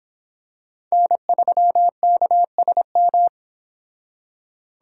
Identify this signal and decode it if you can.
Morse code, heard as a keyed tone, about 26 wpm, 700 Hz. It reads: N3KSM